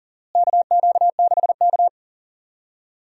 Morse code, heard as a keyed tone, about 38 wpm, 700 Hz. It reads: KQ6K
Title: USA Call Signs - 4 Characters - Repeated 2x - 40wpm